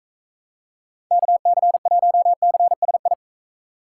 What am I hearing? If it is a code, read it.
Morse code, heard as a keyed tone, about 43 wpm, 700 Hz. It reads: KC1CSI